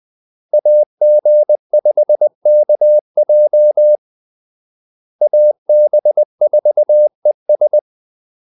Morse code, heard as a keyed tone, about 20 wpm, 600 Hz. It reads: AG5KJ AB4ES
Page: Kaldesignaler 20 wpm